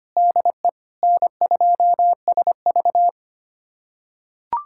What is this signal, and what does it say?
Morse code, heard as a keyed tone, about 25 wpm, 700 Hz. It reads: DE N2SV
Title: USA Callsign DE Callsign - 4 Characters - 25wpm